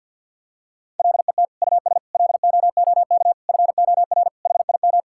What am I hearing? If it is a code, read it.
Morse code, heard as a keyed tone, about 50 wpm, 700 Hz. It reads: LET US LOOK FOR HIM
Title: Sentences from Top 1600 Words 50wpm